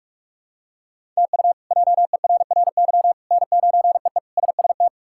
Morse code, heard as a keyed tone, about 46 wpm, 700 Hz. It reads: TU JERRY N9EE SST
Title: K1USN SST QSO Practice - 45wpm